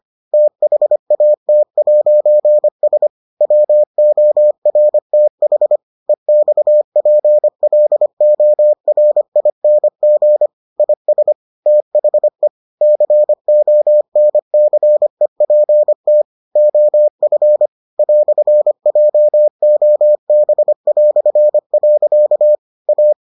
Morse code, heard as a keyed tone, about 25 wpm, 600 Hz. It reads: THAT'S WORTH EXPLORING IS THE CONCEPT OF "JOB". A